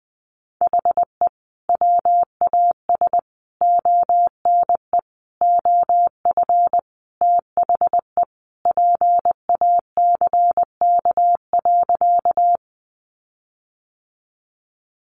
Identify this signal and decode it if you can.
Morse code, heard as a keyed tone, about 20 wpm, 700 Hz. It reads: HE WAS ONE OF THE PACK.
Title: Sentences from Top 1300 Words 20wpm